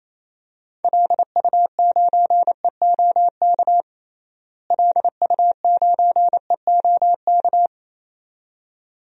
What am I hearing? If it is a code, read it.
Morse code, heard as a keyed tone, about 28 wpm, 700 Hz. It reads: LU9EOK LU9EOK
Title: DXCC-100 Call Signs - 6 Characters - Repeated 2x - 28wpm